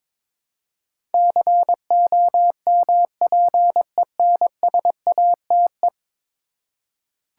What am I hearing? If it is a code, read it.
Morse code, heard as a keyed tone, about 22 wpm, 700 Hz. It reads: COMPENSATE
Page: morse-code-ninja-practice-22wpm